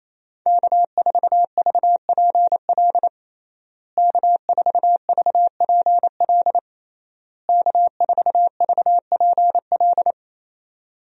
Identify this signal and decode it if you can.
Morse code, heard as a keyed tone, about 28 wpm, 700 Hz. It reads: K4VPL K4VPL K4VPL
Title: USA Call Signs - 5 Characters - Repeated 3x - 28wpm